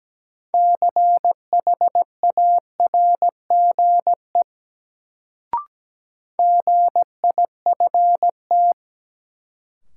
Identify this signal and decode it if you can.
Morse code, heard as a keyed tone, about 17 wpm, 700 Hz. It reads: CHARGE GIFT
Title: Sentences from Top 900 Words - New Word Drill - 17wpm